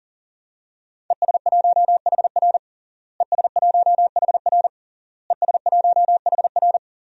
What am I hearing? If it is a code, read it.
Morse code, heard as a keyed tone, about 40 wpm, 700 Hz. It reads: ES1HR ES1HR ES1HR
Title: DXCC-100 Call Signs - 5 Characters - Repeated 3x - 40wpm